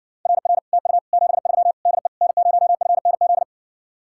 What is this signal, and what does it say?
Morse code, heard as a keyed tone, about 60 wpm, 700 Hz. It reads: RR TU 73 DE N0FNZ